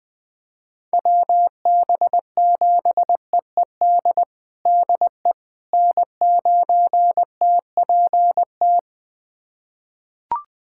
Morse code, heard as a keyed tone, about 21 wpm, 700 Hz. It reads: WB7EED DE N9TPT
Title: USA Callsign DE Callsign 20wpm